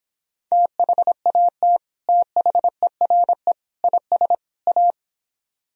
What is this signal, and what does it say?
Morse code, heard as a keyed tone, about 26 wpm, 700 Hz. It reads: THAT THERE IS A
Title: Speed-Racing - Sets of 4 Words 22wpm